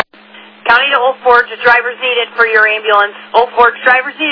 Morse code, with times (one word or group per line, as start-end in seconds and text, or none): none